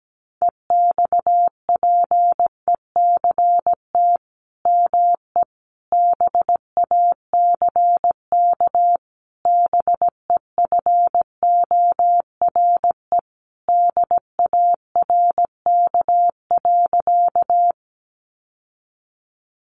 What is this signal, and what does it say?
Morse code, heard as a keyed tone, about 17 wpm, 700 Hz. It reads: EXPECT ME BACK BEFORE DARK.
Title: Sentences from Top 1100 Words 17wpm